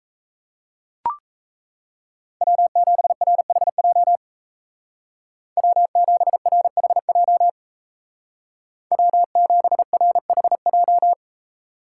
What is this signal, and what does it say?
Morse code, heard as a keyed tone, about 34 wpm, 700 Hz. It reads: W7RHJ W7RHJ W7RHJ
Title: Speed Racing - USA 1x3 Call Signs 28wpm